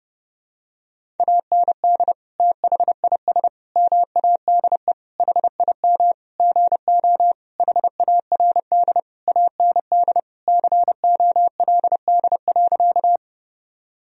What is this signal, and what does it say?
Morse code, heard as a keyed tone, about 30 wpm, 700 Hz. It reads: AND THIS MADE HIM GO HARD AND COLD.
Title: Sentences from Top 1000 Words - Review Part 1 - 30wpm